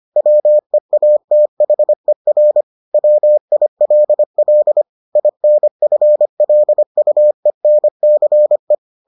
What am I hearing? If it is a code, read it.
Morse code, heard as a keyed tone, about 25 wpm, 600 Hz. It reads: WEATHER WILL INFLUENCE